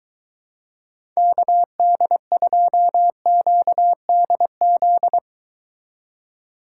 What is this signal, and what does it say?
Morse code, heard as a keyed tone, about 23 wpm, 700 Hz. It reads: KD2QDZ